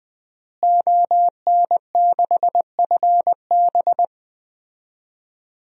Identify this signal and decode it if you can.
Morse code, heard as a keyed tone, about 20 wpm, 700 Hz. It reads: ON6FB
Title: DXCC-100 Call Signs - 5 Characters - Repeated 3x - 20wpm